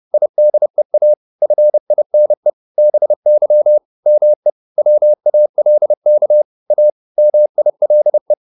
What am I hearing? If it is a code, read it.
Morse code, heard as a keyed tone, about 30 wpm, 600 Hz. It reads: IDEA FINE BY ME WALK A MILE